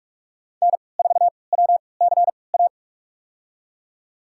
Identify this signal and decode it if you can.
Morse code, heard as a keyed tone, about 46 wpm, spaced as for 25 wpm, 700 Hz. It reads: N4WCA
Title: USA Call Signs - 5 Characters - 3x Character Spacing - 45wpm